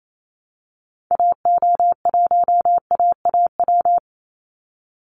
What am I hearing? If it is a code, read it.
Morse code, heard as a keyed tone, about 28 wpm, 700 Hz. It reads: AO1AAW
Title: Call Signs - 28wpm